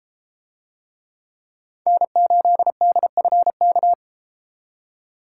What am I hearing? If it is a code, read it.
Morse code, heard as a keyed tone, about 32 wpm, 700 Hz. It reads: N8DFK